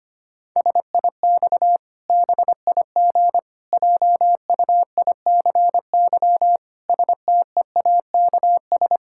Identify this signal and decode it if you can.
Morse code, heard as a keyed tone, about 25 wpm, 700 Hz. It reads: SIX BIG JUICY STEAKS
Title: Pangram Challenge v2 - 25wpm